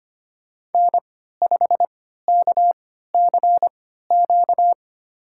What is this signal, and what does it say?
Morse code, heard as a keyed tone, about 25 wpm, spaced as for 14 wpm, 700 Hz. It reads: N5KCQ
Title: USA Call Signs - 3x Character Spacing - 25wpm